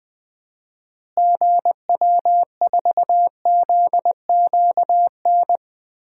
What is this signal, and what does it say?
Morse code, heard as a keyed tone, about 20 wpm, 700 Hz. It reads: GW4ZQN
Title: DXCC-100 Call Signs - 6 Characters - Repeated 3x - 20wpm